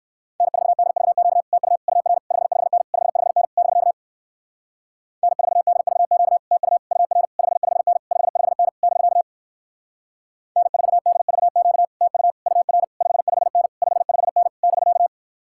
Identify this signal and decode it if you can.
Morse code, heard as a keyed tone, about 61 wpm, 700 Hz. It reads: N4DVQ TU UR 55N 55N <BK> N4DVQ TU UR 55N 55N <BK> N4DVQ TU UR 55N 55N <BK>